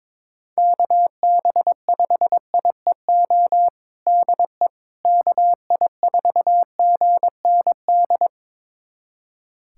Morse code, heard as a keyed tone, about 22 wpm, 700 Hz. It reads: KB5IEO DE KI4GND
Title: USA Callsign DE Callsign 22wpm